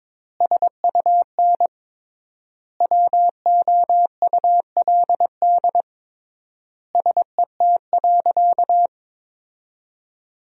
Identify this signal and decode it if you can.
Morse code, heard as a keyed tone, about 22 wpm, 700 Hz. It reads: SUN WOULD SET.
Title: 3x Word-Spacing - Sentences from Top 400 Words - 22wpm